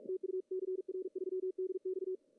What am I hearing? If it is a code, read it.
Morse code, heard as a keyed tone, about 41 wpm, 370 Hz, with radio noise. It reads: TU CR3DX
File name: CW / cqww-cw-k5tr